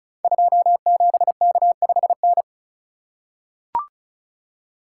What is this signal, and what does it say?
Morse code, heard as a keyed tone, about 36 wpm, 700 Hz. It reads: 27K5N
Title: Code Groups - 5 Letter:Numbers - 35wpm